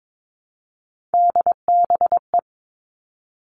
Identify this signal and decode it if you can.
Morse code, heard as a keyed tone, about 22 wpm, 700 Hz. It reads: DBE
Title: All Cut Numbers - 3-Digit 22wpm